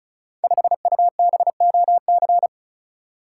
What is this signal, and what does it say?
Morse code, heard as a keyed tone, about 36 wpm, 690 Hz. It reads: 5UBOC
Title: Code Groups - 5 Letter:Numbers - 35wpm